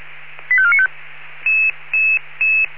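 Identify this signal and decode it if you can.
Five-tone selective call (ZVEI), heard as quick runs of five tones, tones of about 70 ms. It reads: 85495
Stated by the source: ZVEI five-tone call received on SDR.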